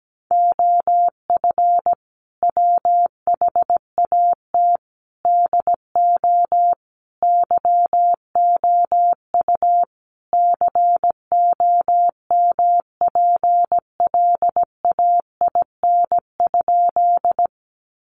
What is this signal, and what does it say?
Morse code, heard as a keyed tone, about 17 wpm, 700 Hz. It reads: OF WHAT DO YOU COMPLAIN?
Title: Sentences from Top 1300 Words 17wpm